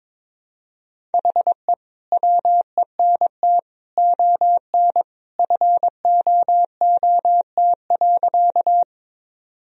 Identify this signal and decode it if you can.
Morse code, heard as a keyed tone, about 22 wpm, 700 Hz. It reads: HE WENT ON FOOT.
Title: Sentences from Top 1000 Words - Review Part 3 - 22wpm